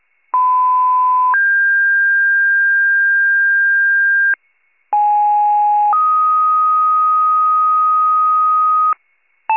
Two-tone page, heard as a one-second tone, then a three-second tone, about 1010 then 1600 Hz, about 830 then 1250 Hz, with noise. It is heard from 0.3 to 8.9 s.